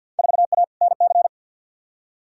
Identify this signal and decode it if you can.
Morse code, heard as a keyed tone, about 47 wpm, 700 Hz. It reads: 4A NC
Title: Field Day Exchanges - 50wpm